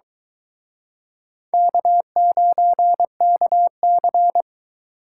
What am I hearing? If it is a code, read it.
Morse code, heard as a keyed tone, about 24 wpm, 700 Hz. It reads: K9KC